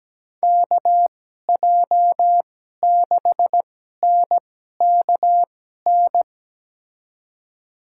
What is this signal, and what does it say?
Morse code, heard as a keyed tone, about 17 wpm, spaced as for 12 wpm, 700 Hz. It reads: KJ6NKN
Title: USA Call Signs - 6 Characters - 2x Character Spacing - 17wpm